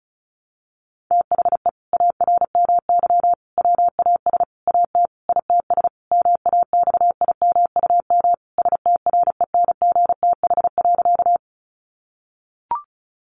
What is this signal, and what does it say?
Morse code, heard as a keyed tone, about 34 wpm, 700 Hz. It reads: THE ARMY WAS AT ITS MAXIMUM STRENGTH.
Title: Sentences from Top 1300 Words 35wpm